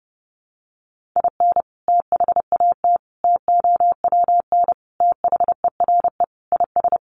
Morse code, heard as a keyed tone, about 30 wpm, 700 Hz. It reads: IN THAT TOWN THERE IS
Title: Sentences from Top 500 Words 30wpm